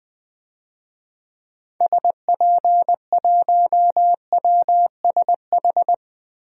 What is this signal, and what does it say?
Morse code, heard as a keyed tone, about 20 wpm, 700 Hz. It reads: SP1WSH
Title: DXCC-100 Call Signs - 6 Characters - Repeated 3x - 20wpm